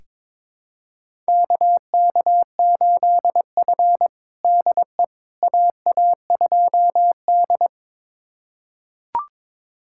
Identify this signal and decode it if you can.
Morse code, heard as a keyed tone, about 22 wpm, 700 Hz. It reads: KK8F DE AA2D
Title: USA Callsign DE Callsign - 4 Characters - 22wpm